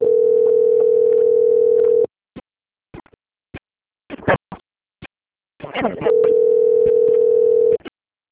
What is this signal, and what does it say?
Ringback tone (North America), 440 + 480 Hz, with speech.
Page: Jacarei - SP